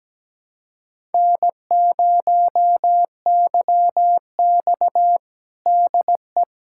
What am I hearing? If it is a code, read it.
Morse code, heard as a keyed tone, about 17 wpm, 700 Hz. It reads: N0YX DE
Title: USA Callsign DE Callsign - 4 Characters - 17wpm